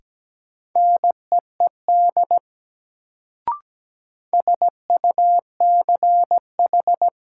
Morse code, heard as a keyed tone, about 17 wpm, 700 Hz. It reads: NEED SUCH
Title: Sentences from Top 200 Words - New Word Drill - 17wpm